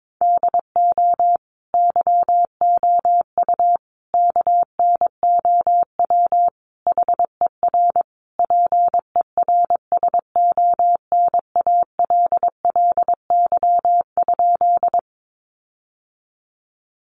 Morse code, heard as a keyed tone, about 22 wpm, 700 Hz. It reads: DO YOU KNOW HER PERSONALLY?
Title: Sentences from Top 1400 Words 22wpm